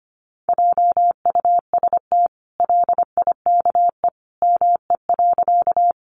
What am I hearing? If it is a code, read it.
Morse code, heard as a keyed tone, about 25 wpm, 700 Hz. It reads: JUST LIKE ME.
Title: Sentences from Top 200 Words 25wpm